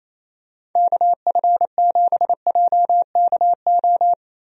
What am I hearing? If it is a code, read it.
Morse code, heard as a keyed tone, about 28 wpm, 700 Hz. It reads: KF7JKO
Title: USA Call Signs - 6 Characters - Repeated 2x - 28wpm